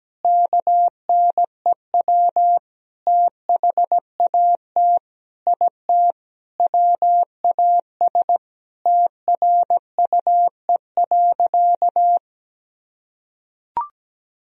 Morse code, heard as a keyed tone, about 17 wpm, 700 Hz. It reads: KNEW THAT IT WAS TRUE.
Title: Sentences from Top 400 Words and Abbreviations 17wpm